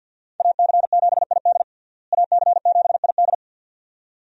Morse code, heard as a keyed tone, about 51 wpm, 700 Hz. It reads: AC7ID AC7ID
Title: USA Call Signs - 5 Characters - Repeated 2x - 50wpm